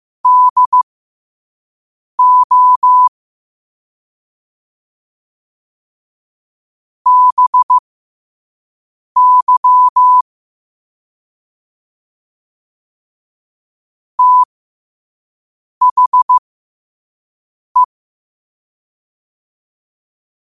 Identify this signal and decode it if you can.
Morse code, heard as a keyed tone, about 15 wpm, spaced as for 5 wpm, 1000 Hz. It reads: DO BY THE
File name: qotdfiles / 5wpm